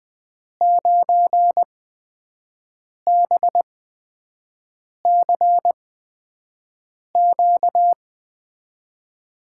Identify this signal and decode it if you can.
Morse code, heard as a keyed tone, about 20 wpm, spaced as for 5 wpm, 700 Hz. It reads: 9BCQ
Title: USA Call Signs - 5 Characters - 8x Character Spacing - 20wpm